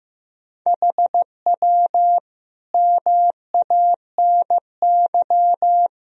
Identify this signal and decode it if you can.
Morse code, heard as a keyed tone, about 15 wpm, 700 Hz. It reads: HW MANY
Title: Sentences from Top 400 Words and Abbreviations 15wpm